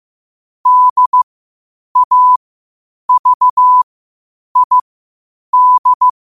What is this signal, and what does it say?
Morse code, heard as a keyed tone, about 15 wpm, spaced as for 8 wpm, 1000 Hz. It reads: DAVID
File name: qotdfiles / 8wpm